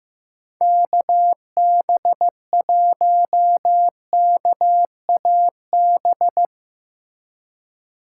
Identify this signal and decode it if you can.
Morse code, heard as a keyed tone, about 15 wpm, 700 Hz. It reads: KB1KAB
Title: USA Call Signs - 6 Characters - 3x Character Spacing - 15wpm